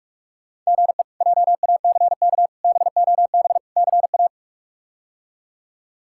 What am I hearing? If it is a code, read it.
Morse code, heard as a keyed tone, about 42 wpm, 700 Hz. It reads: GE JACK BOB CA
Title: K1USN SST QSO Practice - 45wpm